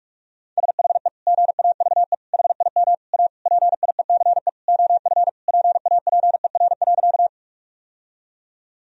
Morse code, heard as a keyed tone, about 46 wpm, 700 Hz. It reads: SHE GAVE HIM A PIECE OF PAPER.
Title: Sentences from Top 1000 Words - Review Part 2 - 45wpm